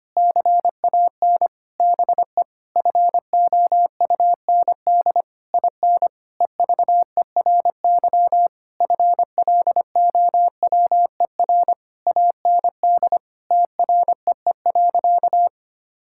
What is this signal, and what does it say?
Morse code, heard as a keyed tone, about 25 wpm, 700 Hz. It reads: CAN BE FOUND IN EVERY FLOWER AND TREE.
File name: practice / 25wpm / Difficult Sentences with Hints - 25wpm